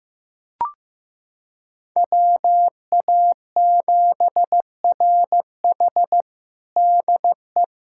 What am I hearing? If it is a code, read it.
Morse code, heard as a keyed tone, about 15 wpm, 700 Hz. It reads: WA7RH DE
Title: USA Callsign DE Callsign 15wpm